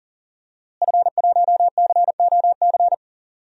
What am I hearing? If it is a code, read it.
Morse code, heard as a keyed tone, about 37 wpm, 700 Hz. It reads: F1COC
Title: DXCC-100 Call Signs - 5 Characters - Repeated 2x - 40wpm